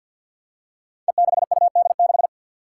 Morse code, heard as a keyed tone, about 52 wpm, 700 Hz. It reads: E6UD6
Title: Five Challenging Letter-Number Combinations 50wpm